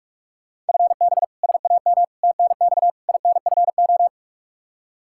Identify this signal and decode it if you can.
Morse code, heard as a keyed tone, about 46 wpm, 700 Hz. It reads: FB SAM TNX INFO
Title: Practice QSO - Exchange 3 - More Info - 45wpm